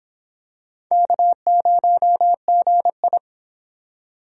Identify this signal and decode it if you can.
Morse code, heard as a keyed tone, about 25 wpm, 700 Hz. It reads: K0GI